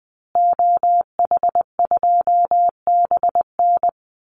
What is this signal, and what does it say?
Morse code, heard as a keyed tone, about 20 wpm, 700 Hz. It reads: OH2BN
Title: Call Signs Repeated 2x - 20wpm